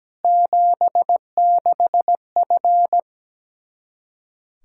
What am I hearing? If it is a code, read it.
Morse code, heard as a keyed tone, about 17 wpm, 700 Hz. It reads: 76F
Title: Practice QSO - Exchange 2 - Info - 17wpm